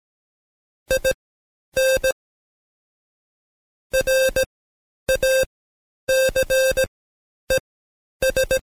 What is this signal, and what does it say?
Morse code, heard as a keyed tone, about 17 wpm, spaced as for 10 wpm, 530 Hz. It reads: IN RACES
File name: Code Quick / AudioDownld2 / Seg131-Stepper4 9WPM